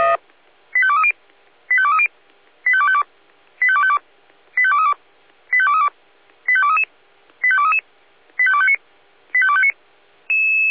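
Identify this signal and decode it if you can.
Five-tone selective call (ZVEI), heard as quick runs of five tones, tones of about 70 ms. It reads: 85230 85230 85252 85252 85232 85232 85233 85233 85259 85259
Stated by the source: ZVEI five-tone selective call 85267, received on an SDR.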